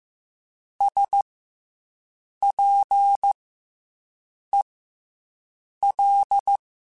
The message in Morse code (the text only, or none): SPEL